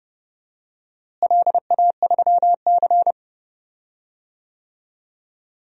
Morse code, heard as a keyed tone, about 31 wpm, 700 Hz. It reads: LA3C
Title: DXCC-100 Call Signs - 4 Characters - Repeated 3x - 30wpm